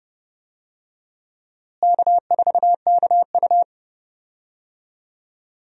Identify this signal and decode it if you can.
Morse code, heard as a keyed tone, about 31 wpm, 700 Hz. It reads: K4KU
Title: USA Call Signs - 4 Characters - Repeated 3x - 30wpm